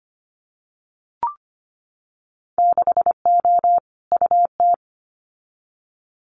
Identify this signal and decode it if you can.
Morse code, heard as a keyed tone, about 25 wpm, 700 Hz. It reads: 6O UT
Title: Winter Field Day Exchanges Encore 25wpm